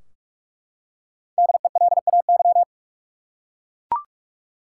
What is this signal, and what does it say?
Morse code, heard as a keyed tone, about 42 wpm, 700 Hz. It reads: DELAY
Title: Five Letter Words - 45wpm